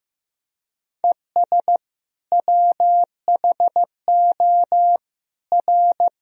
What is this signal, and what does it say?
Morse code, heard as a keyed tone, about 15 wpm, 700 Hz. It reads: ES WHO R
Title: Sentences from Top 100 Words and Abbreviations 15wpm